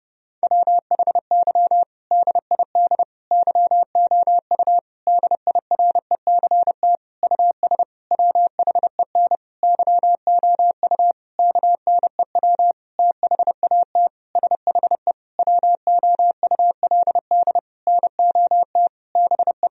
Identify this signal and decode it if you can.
Morse code, heard as a keyed tone, about 30 wpm, 700 Hz. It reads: WHY DID YOU DIRECT US WHEN YOU KNEW THAT SHE WOULD NOT BE
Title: Sentences from Top 1000 Words - Review Part 2 - 30wpm